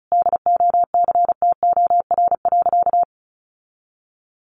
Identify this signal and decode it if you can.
Morse code, heard as a keyed tone, about 34 wpm, 700 Hz. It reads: DOCTOR.
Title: Sentences from Top 1500 Words 35wpm